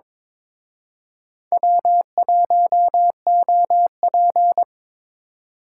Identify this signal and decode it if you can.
Morse code, heard as a keyed tone, about 26 wpm, 700 Hz. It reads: W1OP
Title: Speed Racing - USA 1x2 Call Signs 22wpm